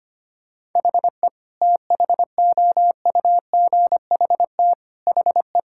Morse code, heard as a keyed tone, about 25 wpm, 700 Hz. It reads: HE THOUGHT HE
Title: Sentences from Top 1000 Words - Review Part 6 - 25wpm